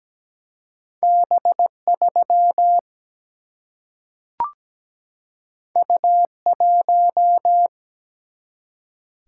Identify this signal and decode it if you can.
Morse code, heard as a keyed tone, about 17 wpm, 700 Hz. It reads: B3 U1
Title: Single-Letter Single-Number 17wpm